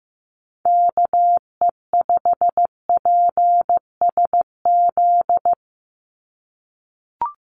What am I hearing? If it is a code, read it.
Morse code, heard as a keyed tone, about 15 wpm, 700 Hz. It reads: KE5PSZ
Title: USA Call Signs 15wpm